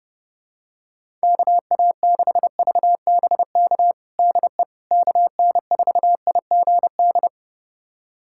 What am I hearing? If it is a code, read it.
Morse code, heard as a keyed tone, about 30 wpm, 700 Hz. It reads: KA6VBK DE KN4IGD
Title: USA Callsign DE Callsign 30wpm